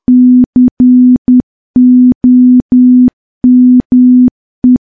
Morse code, heard as a keyed tone, about 10 wpm, 250 Hz. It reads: COME
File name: morse / morse-training-10WPM